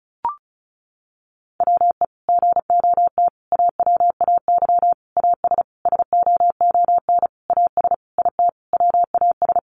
Morse code, heard as a keyed tone, about 34 wpm, 700 Hz. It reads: WE GOT AWAY AS SOON AS IT WAS
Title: Sentences from Top 600 Words 35wpm